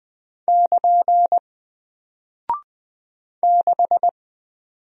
Morse code, heard as a keyed tone, about 20 wpm, 700 Hz. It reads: <KN> 6
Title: Single Character (Letter:Number:Punctuation:Prosign) Alt 20wpm